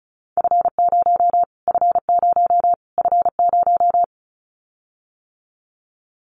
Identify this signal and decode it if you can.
Morse code, heard as a keyed tone, about 34 wpm, 700 Hz. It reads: F0 F0 F0
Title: Callsign Prefixes for Top 25 DXCC - Repeated 3x - 35wpm